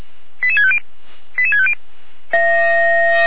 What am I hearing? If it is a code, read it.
Five-tone selective call (ZVEI), heard as quick runs of five tones, tones of about 70 ms. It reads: 88650 88650